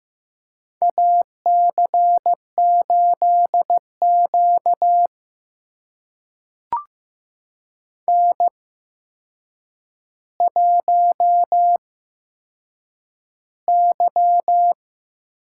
Morse code, heard as a keyed tone, about 15 wpm, spaced as for 7 wpm, 700 Hz. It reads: AC8Q N1Y
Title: USA Call Signs - 8x Character Spacing - 15wpm